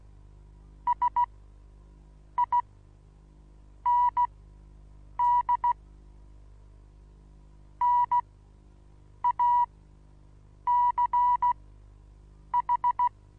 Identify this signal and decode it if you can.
Morse code, heard as a keyed tone, about 15 wpm, spaced as for 6 wpm, 970 Hz. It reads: SIND NACH
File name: Onlinelehrgang / morsen / DJ4UF Morsen25